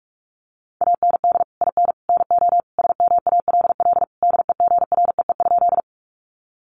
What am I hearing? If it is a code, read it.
Morse code, heard as a keyed tone, about 44 wpm, 700 Hz. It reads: AND IN NO SMALL DEGREE?
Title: Sentences from Top 1500 Words 45wpm